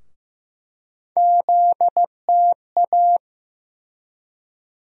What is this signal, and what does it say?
Morse code, heard as a keyed tone, about 15 wpm, 700 Hz. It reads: ZTA
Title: USA Call Sign Postfixes - 3 Letters 15wpm